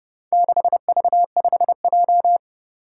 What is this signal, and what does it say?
Morse code, heard as a keyed tone, about 32 wpm, 700 Hz. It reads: 6V5J
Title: Four Challenging Letter-Number Combinations 30wpm